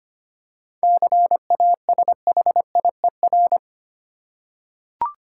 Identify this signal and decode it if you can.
Morse code, heard as a keyed tone, about 25 wpm, 700 Hz. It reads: CASHIER
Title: Practice QSO - Exchange 3 - More Info - 25wpm